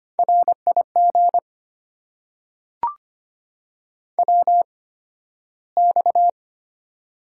Morse code, heard as a keyed tone, about 25 wpm, spaced as for 13 wpm, 700 Hz. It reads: RIG WX
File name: Top 90 QSO Elements - 8x Character Spacing - 25wpm